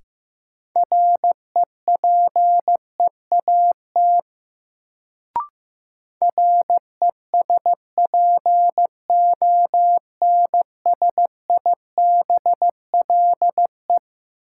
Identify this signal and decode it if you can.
Morse code, heard as a keyed tone, about 15 wpm, 700 Hz. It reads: REPEAT RESPONSIBLE
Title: Sentences from Top 1600 Words - New Word Drill Alt - 15wpm